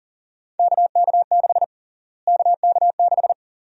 Morse code, heard as a keyed tone, about 37 wpm, 700 Hz. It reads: KK6 KK6
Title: USA Call Sign Prefixes - Repeated 2x - 40wpm